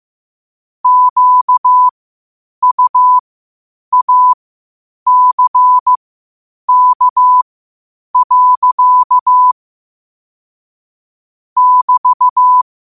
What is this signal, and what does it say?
Morse code, heard as a keyed tone, about 15 wpm, spaced as for 8 wpm, 1000 Hz. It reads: QUACK. <BT>
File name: qotdfiles / 8wpm